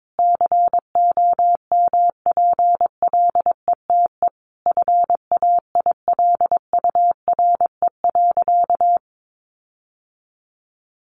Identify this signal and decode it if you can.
Morse code, heard as a keyed tone, about 22 wpm, 700 Hz. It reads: COMPLETE FAILURE.
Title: Sentences from Top 1400 Words 22wpm